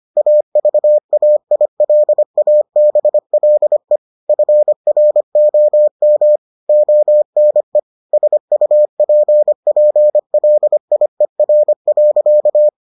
Morse code, heard as a keyed tone, about 25 wpm, 600 Hz. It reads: AVAILABLE FROM ONE SUPPLIER.